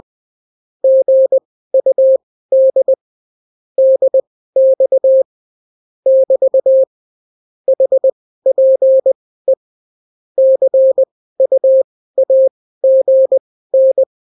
Morse code, heard as a keyed tone, about 20 wpm, spaced as for 14 wpm, 530 Hz. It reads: GUD DX <BT> HPE CUAGN
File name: CW QSO libroCW4 20-10ppm